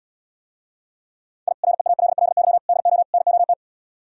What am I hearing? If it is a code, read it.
Morse code, heard as a keyed tone, about 77 wpm, 700 Hz. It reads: I REALLY DO NOT